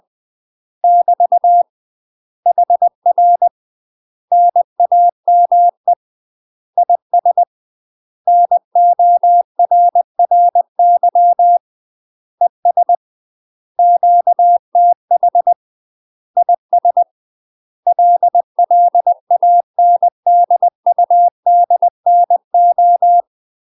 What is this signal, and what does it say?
Morse code, heard as a keyed tone, about 20 wpm, 700 Hz. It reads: <BT> HR NAME IS NORRY ES QTH IS LLANDUDNO
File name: UK MASTER QSOs TRAINING / 20WPM / UK MASTER QSOs 0008